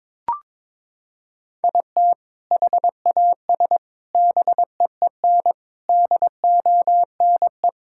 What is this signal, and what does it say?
Morse code, heard as a keyed tone, about 22 wpm, 700 Hz. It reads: IT HAS BEEN DONE
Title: Sentences from Top 1600 Words 22wpm